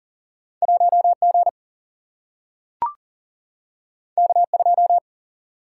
Morse code, heard as a keyed tone, about 39 wpm, 700 Hz. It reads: 1G K2
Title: Single-Letter Single-Number 40wpm